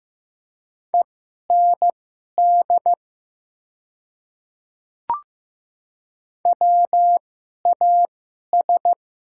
Morse code, heard as a keyed tone, about 15 wpm, spaced as for 11 wpm, 700 Hz. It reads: END WAS
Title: 2x Character Spacing - Top 300 Words - 15wpm